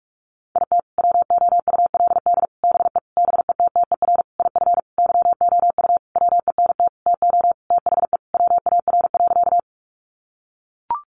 Morse code, heard as a keyed tone, about 44 wpm, 700 Hz. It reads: IT WOULD BE BETTER IF YOU WENT TO THE WAR.
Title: Sentences from Top 400 Words 45wpm